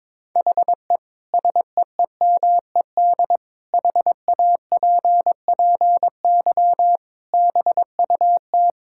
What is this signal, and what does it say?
Morse code, heard as a keyed tone, about 22 wpm, 700 Hz. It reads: HE SEEMED HAPPY BUT
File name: practice / 22wpm / Sentences with 8-Word Phrases - 22wpm